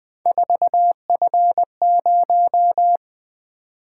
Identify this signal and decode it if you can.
Morse code, heard as a keyed tone, about 21 wpm, 700 Hz. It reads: 4F0
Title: Three Letter-Number Combinations - 20wpm